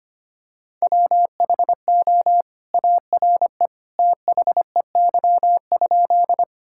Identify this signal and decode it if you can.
Morse code, heard as a keyed tone, about 25 wpm, 700 Hz. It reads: WHO ARE THEY?
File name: practice / 25wpm / Sentences with 2-Word Phrases - 25wpm